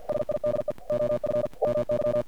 Morse code, heard as a keyed tone, about 50 wpm, 600 Hz. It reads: SIDE OF WY
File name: archived / CRSnet